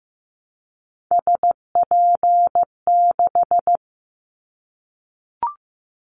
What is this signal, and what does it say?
Morse code, heard as a keyed tone, about 15 wpm, 700 Hz. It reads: SP6
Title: Callsign Prefixes for Top 25 DXCC - 15wpm